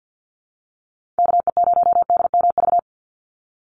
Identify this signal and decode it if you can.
Morse code, heard as a keyed tone, about 50 wpm, 700 Hz. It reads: KE0DMV
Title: USA Call Signs 50wpm